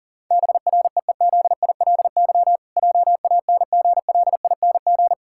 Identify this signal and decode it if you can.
Morse code, heard as a keyed tone, about 40 wpm, 700 Hz. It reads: BREEZILY JANGLING
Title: Pangram Challenge v2 - 40wpm